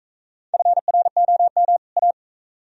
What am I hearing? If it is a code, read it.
Morse code, heard as a keyed tone, about 40 wpm, 690 Hz. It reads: FROM A